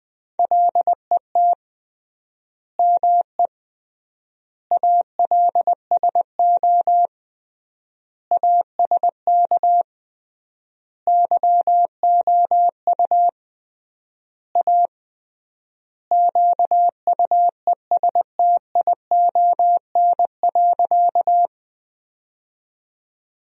Morse code, heard as a keyed tone, about 21 wpm, 700 Hz. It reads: LET ME ALSO ASK YOU A QUESTION.
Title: 3x Word-Spacing - Sentences from Top 400 Words - 20wpm